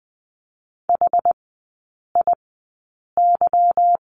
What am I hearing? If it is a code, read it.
Morse code, heard as a keyed tone, about 20 wpm, spaced as for 8 wpm, 700 Hz. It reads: HIY
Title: 152 Lesson 25e - Letter p - Three Characters 20wpm